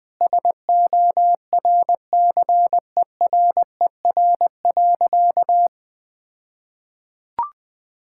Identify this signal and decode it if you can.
Morse code, heard as a keyed tone, about 20 wpm, 700 Hz. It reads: SORCERER.
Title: All-Age Appropriate Jokes - 20wpm